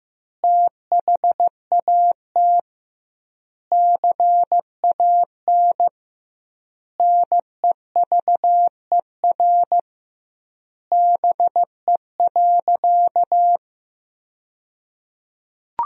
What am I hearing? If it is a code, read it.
Morse code, heard as a keyed tone, about 15 wpm, 700 Hz. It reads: THAT CAN NEVER BE.
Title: 2x Word-Spacing - Sentences from Top 300 Words - 15wpm